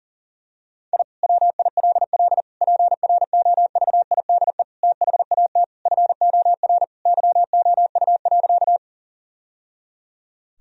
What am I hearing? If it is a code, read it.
Morse code, heard as a keyed tone, about 40 wpm, 700 Hz. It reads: I WILL PROVIDE THAT FOR YOU.
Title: Sentences from Top 1000 Words - Review Part 3 - 40wpm